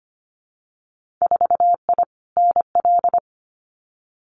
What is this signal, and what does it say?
Morse code, heard as a keyed tone, about 25 wpm, 690 Hz. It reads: 4I NL
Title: Winter Field Day Exchanges Encore 25wpm